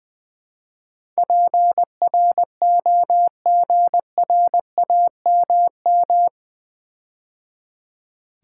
Morse code, heard as a keyed tone, about 20 wpm, 700 Hz. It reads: PROGRAMM
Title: Build A Word - 4 Letters At A Time - 11-Letter Words - 20wpm